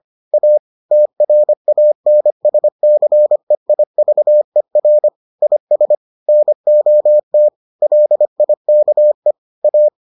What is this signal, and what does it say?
Morse code, heard as a keyed tone, about 25 wpm, 600 Hz. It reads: A TRANSCEIVER IS NOT LIKE A